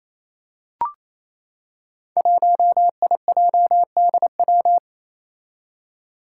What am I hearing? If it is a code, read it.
Morse code, heard as a keyed tone, about 28 wpm, 700 Hz. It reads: 1IJDW
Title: Five Challenging Letter-Number Combinations 28wpm